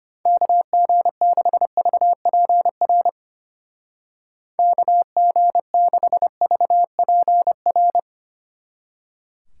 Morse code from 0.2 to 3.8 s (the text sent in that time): KG6VPR